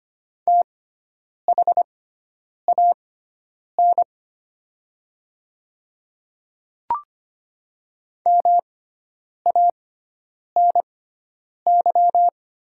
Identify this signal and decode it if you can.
Morse code, heard as a keyed tone, about 25 wpm, spaced as for 9 wpm, 700 Hz. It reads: THAN MANY
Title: 6x Character Spacing - Top 200 Words - 25wpm